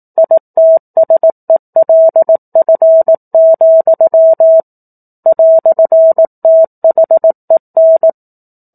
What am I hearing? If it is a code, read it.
Morse code, heard as a keyed tone, about 18 wpm, 650 Hz. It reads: ITSELF, "THEN